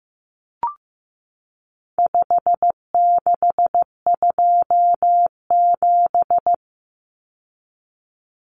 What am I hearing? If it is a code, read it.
Morse code, heard as a keyed tone, about 15 wpm, 700 Hz. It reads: 5627
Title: Common Cut Numbers - 4-Digit 15wpm